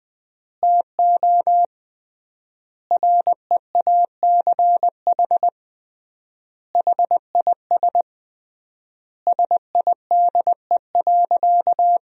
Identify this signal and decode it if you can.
Morse code, heard as a keyed tone, about 20 wpm, 700 Hz. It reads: TO REACH HIS SIDE.
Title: 3x Word-Spacing - Sentences from Top 400 Words - 20wpm